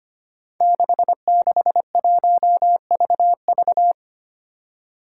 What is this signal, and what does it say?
Morse code, heard as a keyed tone, about 25 wpm, 700 Hz. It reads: 661VV
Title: Five Challenging Letter-Number Combinations 25wpm